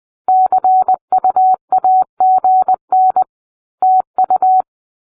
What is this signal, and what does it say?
Morse code, heard as a keyed tone, about 20 wpm, 750 Hz. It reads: CUAGN TU